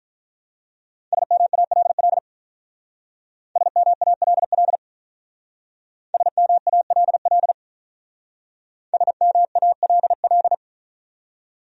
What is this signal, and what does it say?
Morse code, heard as a keyed tone, about 45 wpm, 700 Hz. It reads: SMALL SMALL SMALL SMALL